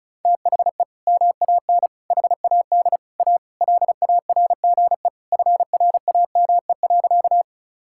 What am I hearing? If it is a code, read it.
Morse code, heard as a keyed tone, about 34 wpm, 700 Hz. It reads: THE MAN HAD A LARGE FRAME.
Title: Sentences from Top 1000 Words - Review Part 6 - 35wpm